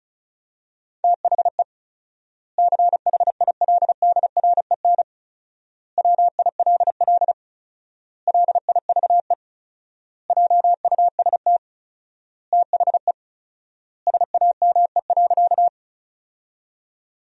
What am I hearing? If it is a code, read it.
Morse code, heard as a keyed tone, about 34 wpm, 700 Hz. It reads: THE CHILDREN WILL LIVE JUST THE SAME.
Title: 4x Word-Spacing - Sentences from Top 300 Words - 35wpm